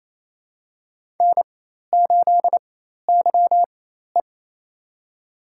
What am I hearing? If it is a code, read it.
Morse code, heard as a keyed tone, about 27 wpm, spaced as for 13 wpm, 700 Hz. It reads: N8YE
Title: USA Call Signs - 4 Characters - 4x Character Spacing - 28wpm